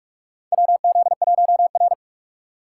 Morse code, heard as a keyed tone, about 43 wpm, 700 Hz. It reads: WZ1R
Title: USA Call Signs - 6x Character Spacing - 45wpm